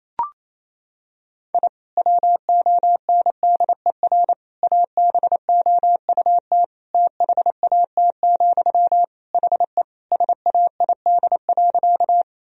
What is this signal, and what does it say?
Morse code, heard as a keyed tone, about 28 wpm, 700 Hz. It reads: I WONDER ABOUT THAT, HE SAID.
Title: Sentences from Top 1000 Words - Review Part 3 - 28wpm